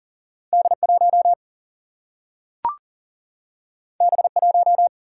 Morse code, heard as a keyed tone, about 40 wpm, 700 Hz. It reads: D1 B1
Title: Single-Letter Single-Number 40wpm